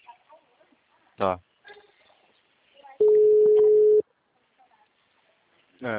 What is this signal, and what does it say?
Ringback tone (Europe, 425 Hz), with speech.